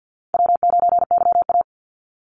Morse code, heard as a keyed tone, about 50 wpm, 700 Hz. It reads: W9YA
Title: Easy Call Signs - 50wpm